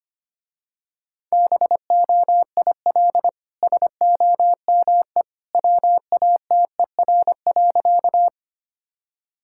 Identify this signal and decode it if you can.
Morse code, heard as a keyed tone, about 25 wpm, 700 Hz. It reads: BOIL SOME WATER.
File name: Easy Sentences with Hints - 25wpm